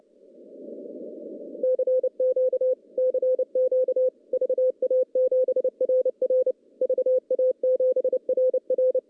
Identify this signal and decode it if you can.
Morse code, heard as a keyed tone, about 29 wpm, 500 Hz. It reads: CQ CQ VA7RR VA7RR
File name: contests / CW-SS-k5tr- -0900